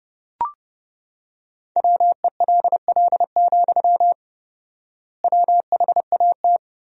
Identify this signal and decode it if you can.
Morse code, heard as a keyed tone, about 30 wpm, 700 Hz. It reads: WELL, WHAT
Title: 4x Word-Spacing - Sentences from Top 300 Words - 30wpm